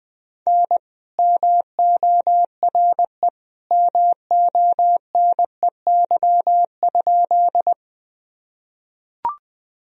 Morse code, heard as a keyed tone, about 20 wpm, 700 Hz. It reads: N MORE MONEY?
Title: Sentences from Top 400 Words and Abbreviations 20wpm